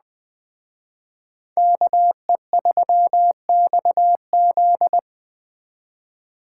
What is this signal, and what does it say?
Morse code, heard as a keyed tone, about 21 wpm, 700 Hz. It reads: KE3XZ
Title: Speed Racing - USA 2x2 Call Signs 17wpm